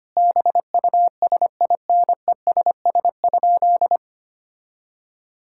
Morse code, heard as a keyed tone, about 25 wpm, 700 Hz. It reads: BUSINESS?
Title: Sentences from Top 1000 Words - Review Part 6 - 25wpm